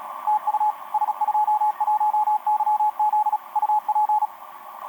Morse code, heard as a keyed tone, about 37 wpm, 850 Hz, with radio noise. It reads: TU S21YZ UP
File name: s21yz ba8ba etc 12m cw 22mar2011 0925